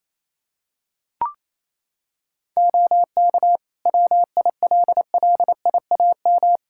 Morse code, heard as a keyed tone, about 28 wpm, 700 Hz. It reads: OK WILLIAM
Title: Practice QSO - Exchange 3 - More Info - 28wpm